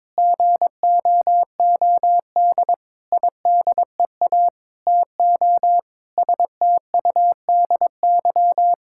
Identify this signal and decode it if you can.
Morse code, heard as a keyed tone, about 22 wpm, 700 Hz. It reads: GOOD IDEA TO STUDY
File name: Sentences with 7-Word Phrases - 22wpm